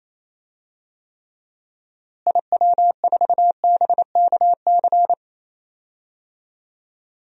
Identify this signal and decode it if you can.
Morse code, heard as a keyed tone, about 28 wpm, 700 Hz. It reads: IW4BKC
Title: DXCC-100 Call Signs - 6 Characters - Repeated 2x - 28wpm